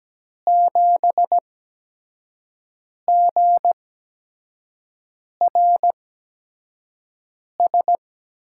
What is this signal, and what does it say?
Morse code, heard as a keyed tone, about 17 wpm, spaced as for 5 wpm, 700 Hz. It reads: 7GRS
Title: USA Call Signs - 6 Characters - 8x Character Spacing - 17wpm